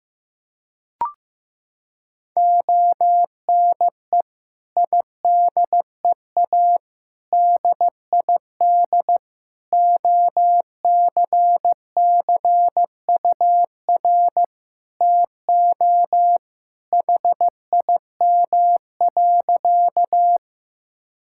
Morse code, heard as a keyed tone, about 15 wpm, 700 Hz. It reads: ONE IDEA DID OCCUR TO HIM.
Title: Sentences from Top 1000 Words - Review Part 6 - 15wpm